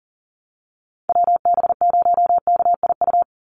Morse code, heard as a keyed tone, about 40 wpm, 700 Hz. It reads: WB0KIU
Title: NA SOTA Chaser Callsigns 40wpm